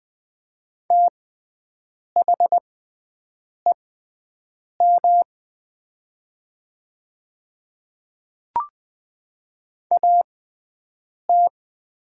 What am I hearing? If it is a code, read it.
Morse code, heard as a keyed tone, about 21 wpm, spaced as for 7 wpm, 700 Hz. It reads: THEM AT
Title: 6x Character Spacing - Top 300 Words - 20wpm